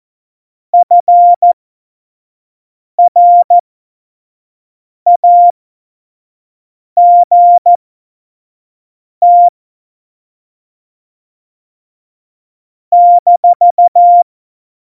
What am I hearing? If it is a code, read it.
Morse code, heard as a keyed tone, about 14 wpm, spaced as for 5 wpm, 700 Hz. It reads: FRAGT -
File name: Morsekurs / Witze / Witz 15. Oktober 2019, 25 F70